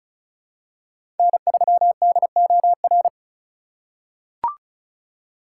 Morse code, heard as a keyed tone, about 36 wpm, 700 Hz. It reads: N3DOR
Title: USA Call Signs - 5 Characters - 4x Character Spacing - 35wpm